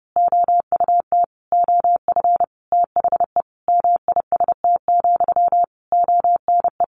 Morse code, heard as a keyed tone, about 30 wpm, 700 Hz. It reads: OUT OF THE MIST, ONE
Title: A Sound Of Thunder 30wpm